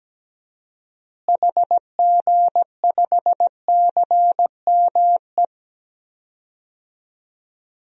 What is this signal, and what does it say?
Morse code, heard as a keyed tone, about 17 wpm, 700 Hz. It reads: HG5CME
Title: DXCC-100 Call Signs - 6 Characters - Repeated 3x - 17wpm